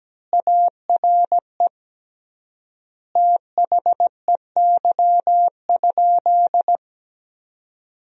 Morse code, heard as a keyed tone, about 17 wpm, 700 Hz. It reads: ARE THEY?
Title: 3x Word-Spacing - Sentences from Top 200 Words - 17wpm